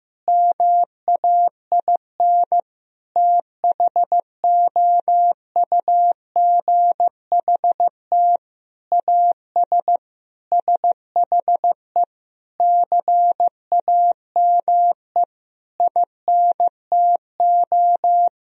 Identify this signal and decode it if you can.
Morse code, heard as a keyed tone, about 15 wpm, 700 Hz. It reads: MAIN THOUGHT AS SHE CAME INTO
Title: Sentences from Top 1000 Words - Review Part 2 - 15wpm